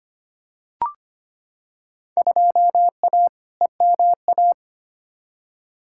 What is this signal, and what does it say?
Morse code, heard as a keyed tone, about 25 wpm, 700 Hz. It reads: 2A EMA
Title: Field Day Exchanges Encore - 25wpm